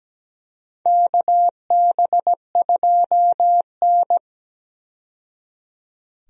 Morse code, heard as a keyed tone, about 17 wpm, 700 Hz. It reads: KB2N
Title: USA Call Signs - 4 Characters - 17wpm